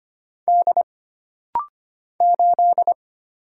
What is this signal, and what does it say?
Morse code, heard as a keyed tone, about 25 wpm, 700 Hz. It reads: D 8
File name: Single Character (Letter:Number:Punctuation:Prosign) - Rapid-Fire - 25wpm